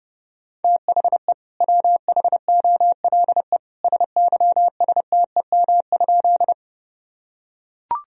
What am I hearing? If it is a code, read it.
Morse code, heard as a keyed tone, about 30 wpm, 700 Hz. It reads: THE WHOLE SYSTEM?
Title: Sentences from Top 1000 Words - Review Part 3 - 30wpm